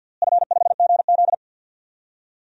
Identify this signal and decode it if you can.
Morse code, heard as a keyed tone, about 46 wpm, 700 Hz. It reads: F5GZ
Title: DXCC-100 Call Signs - 4 Characters - Repeated 3x - 50wpm